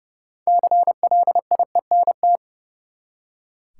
Morse code, heard as a keyed tone, about 30 wpm, 700 Hz. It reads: CLIENT
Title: Six Letter Words - 30wpm